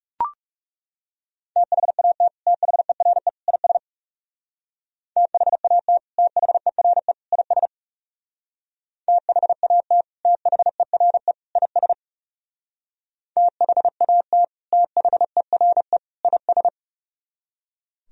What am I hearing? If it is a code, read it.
Morse code, heard as a keyed tone, about 40 wpm, 700 Hz. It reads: THAT THERE IS THAT THERE IS THAT THERE IS THAT THERE IS